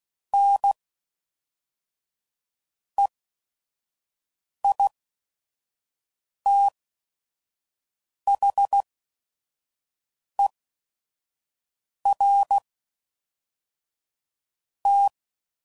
Morse code, heard as a keyed tone, about 16 wpm, spaced as for 5 wpm, 800 Hz. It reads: NEITHER T
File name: CW Code Groups @5WPM / Technical Subjects / AND Gate